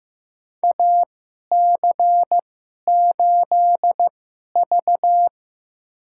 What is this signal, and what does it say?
Morse code, heard as a keyed tone, about 15 wpm, spaced as for 11 wpm, 700 Hz. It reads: AC8V
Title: USA Call Signs - 4 Characters - 2x Character Spacing - 15wpm